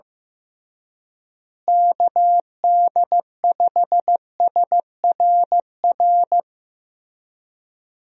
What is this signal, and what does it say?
Morse code, heard as a keyed tone, about 15 wpm, 700 Hz. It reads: KD5SRR
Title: USA Call Signs - 6 Characters - Repeated 3x - 15wpm